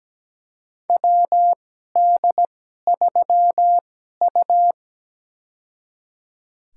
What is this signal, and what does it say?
Morse code, heard as a keyed tone, about 17 wpm, spaced as for 12 wpm, 700 Hz. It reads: WD3U
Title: USA Call Signs - 4 Characters - 2x Character Spacing - 17wpm